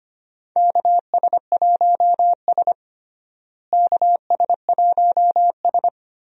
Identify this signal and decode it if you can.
Morse code, heard as a keyed tone, about 25 wpm, 700 Hz. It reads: KS1S KS1S
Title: USA Call Signs - 4 Characters - Repeated 2x - 25wpm